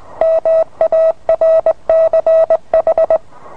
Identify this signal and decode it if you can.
Morse code, heard as a keyed tone, about 20 wpm, 650 Hz. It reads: MARCH